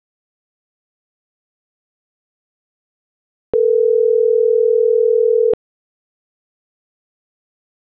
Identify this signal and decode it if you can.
Ringback tone (North America), 440 + 480 Hz.